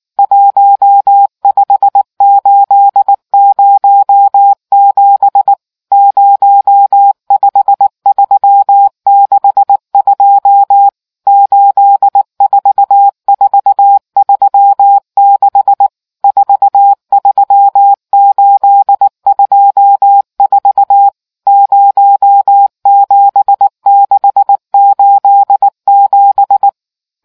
Morse code, heard as a keyed tone, about 19 wpm, 800 Hz. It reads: 15807 05362 84436 43824 07687